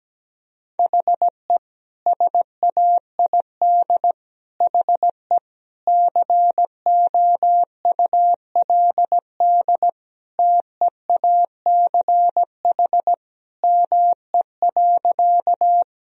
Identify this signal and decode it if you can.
Morse code, heard as a keyed tone, about 17 wpm, 700 Hz. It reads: HE SAID HE COULD TEACH ME.
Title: Sentences from Top 1000 Words - Review Part 2 - 17wpm